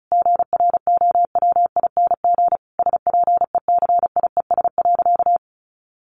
Morse code, heard as a keyed tone, about 34 wpm, 700 Hz. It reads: GROWING SPECIES.
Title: A Sound Of Thunder 35wpm